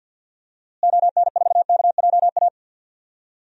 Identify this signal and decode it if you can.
Morse code, heard as a keyed tone, about 52 wpm, 700 Hz. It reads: ON4KJA
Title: DXCC-100 Call Signs - 6 Characters - Repeated 2x - 50wpm